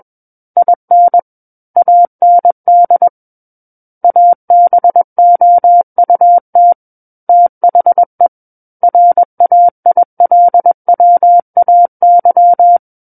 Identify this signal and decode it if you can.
Morse code, heard as a keyed tone, about 21 wpm, 700 Hz. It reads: IN AND ABOUT THE RAILWAY
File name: Book One / 12-Chapter 12 - 21 WPM